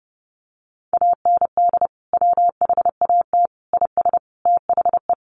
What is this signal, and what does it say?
Morse code, heard as a keyed tone, about 30 wpm, 700 Hz. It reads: AND WHAT IS THE
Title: Sentences from Top 800 Words 30wpm